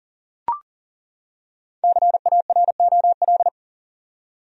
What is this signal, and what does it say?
Morse code, heard as a keyed tone, about 37 wpm, 700 Hz. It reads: CAROL
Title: USA Common Names - Max 5 Letters - 40wpm